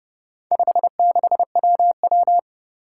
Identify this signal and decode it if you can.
Morse code, heard as a keyed tone, about 31 wpm, 700 Hz. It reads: 56WW
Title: Four Challenging Letter-Number Combinations 30wpm